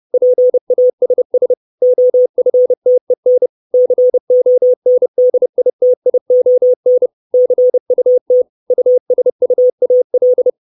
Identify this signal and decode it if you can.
Morse code, heard as a keyed tone, about 30 wpm, 500 Hz. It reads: PASS OFTEN CONDITION CUT USUAL
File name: 632 words / 632 Words 30WPM